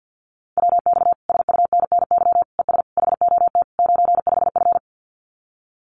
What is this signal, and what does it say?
Morse code, heard as a keyed tone, about 50 wpm, 700 Hz. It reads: WX SUNNY ES HOT 95F
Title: Common QSO Phrases Encore 50wpm